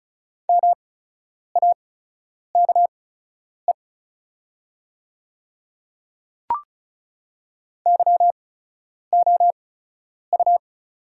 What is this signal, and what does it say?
Morse code, heard as a keyed tone, about 36 wpm, spaced as for 10 wpm, 700 Hz. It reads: MAKE YOU
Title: 8x Character Spacing - Top 100 Words - 35wpm